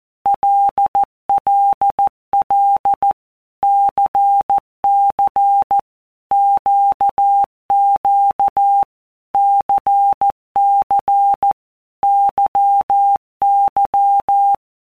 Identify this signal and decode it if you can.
Morse code, heard as a keyed tone, about 14 wpm, 800 Hz. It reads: LLL CC QQ CC YY
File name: Basic Letter Groups By WPM's / Groups@13 / LFCYQ@13